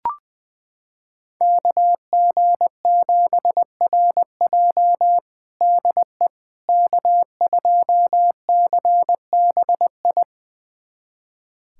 Morse code, heard as a keyed tone, about 20 wpm, 700 Hz. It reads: KG7RJ DE K2CBI
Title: USA Callsign DE Callsign - 5 Characters - 20wpm